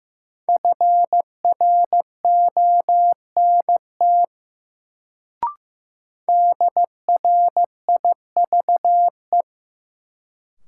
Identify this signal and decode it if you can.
Morse code, heard as a keyed tone, about 15 wpm, 700 Hz. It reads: FRONT DRIVE
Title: Sentences from Top 500 Words - New Word Drill - 15wpm